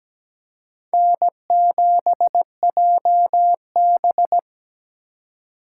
Morse code, heard as a keyed tone, about 17 wpm, 700 Hz. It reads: N7JB
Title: USA Call Signs - 4 Characters - 2x Character Spacing - 17wpm